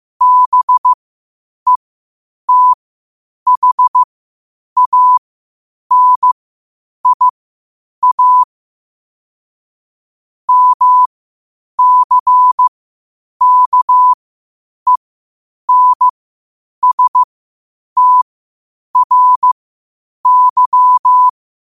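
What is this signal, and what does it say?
Morse code, heard as a keyed tone, about 15 wpm, spaced as for 8 wpm, 1000 Hz. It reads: BETHANIA MCKENSTRY
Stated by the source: Morse code quotes at 8 words per minute from QOTD at 8 WPM.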